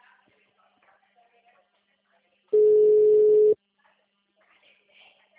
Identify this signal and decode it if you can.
Ringback tone (Europe, 425 Hz).